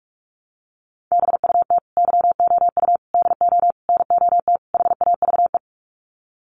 Morse code, heard as a keyed tone, about 44 wpm, 700 Hz. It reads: BUT YOU DO NOT HAVE
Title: Sets of 5 Words Encore 45wpm